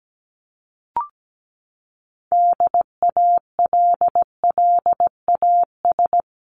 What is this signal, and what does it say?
Morse code, heard as a keyed tone, about 17 wpm, 700 Hz. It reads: DALLAS
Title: CWT Exchange 17wpm